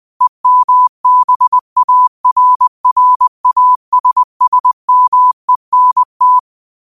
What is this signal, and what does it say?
Morse code, heard as a keyed tone, about 20 wpm, 1000 Hz. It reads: EMBARRASSMENT